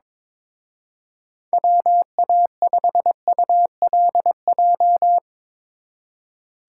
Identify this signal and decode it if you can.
Morse code, heard as a keyed tone, about 22 wpm, 700 Hz. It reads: WA5ULJ
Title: Speed Racing - USA 2x3 Call Signs 22wpm